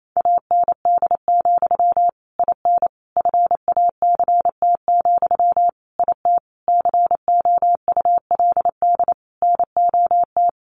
Morse code, heard as a keyed tone, about 28 wpm, 700 Hz. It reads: AND, IN FACT, IT COULD NOT
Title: Sentences from Top 1200 Words 28wpm